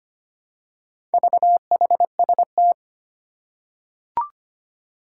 Morse code, heard as a keyed tone, about 25 wpm, 700 Hz. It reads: VHST
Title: Four Challenging Letter-Number Combinations 25wpm